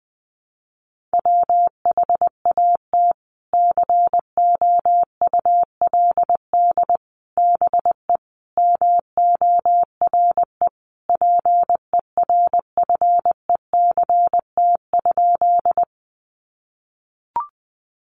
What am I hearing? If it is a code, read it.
Morse code, heard as a keyed tone, about 19 wpm, 700 Hz. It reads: WHAT COULD BE MORE PERFECT?
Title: Sentences from Top 1000 Words 20wpm